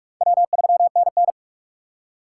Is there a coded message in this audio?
Yes: Morse code from 0.2 to 1.3 s, heard as a keyed tone, about 43 wpm, 700 Hz. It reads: W3NN